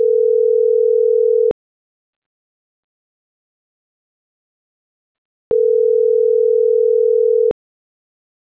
Ringback tone (North America), 440 + 480 Hz.